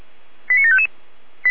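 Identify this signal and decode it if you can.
Five-tone selective call (ZVEI), heard as quick runs of five tones, tones of about 70 ms. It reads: 89755
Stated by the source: ZVEI five-tone selective call received on an SDR.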